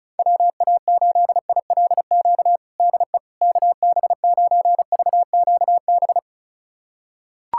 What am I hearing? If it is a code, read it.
Morse code, heard as a keyed tone, about 36 wpm, 700 Hz. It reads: WA8ILQ DE KB9VQB
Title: USA Callsign DE Callsign 35wpm